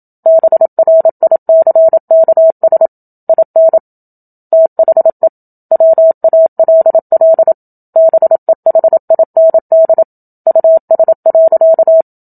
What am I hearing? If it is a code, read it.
Morse code, heard as a keyed tone, about 28 wpm, 650 Hz. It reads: BRICKS IN THE WALL BEHIND US.
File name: Book Two / 01-Chapter 1 - 27 WPM